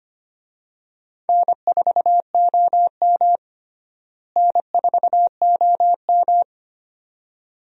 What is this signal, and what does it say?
Morse code, heard as a keyed tone, about 25 wpm, 700 Hz. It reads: N4OM N4OM
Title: USA Call Signs - 4 Characters - Repeated 2x - 25wpm